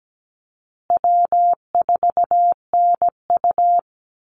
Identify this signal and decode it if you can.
Morse code, heard as a keyed tone, about 17 wpm, 700 Hz. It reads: W4NU
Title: Easy Call Signs Repeated 3x - 17wpm